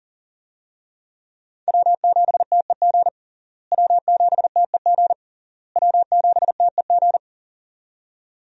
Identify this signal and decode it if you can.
Morse code, heard as a keyed tone, about 40 wpm, 700 Hz. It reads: W7TEG W7TEG W7TEG
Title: USA Call Signs - 5 Characters - Repeated 3x - 40wpm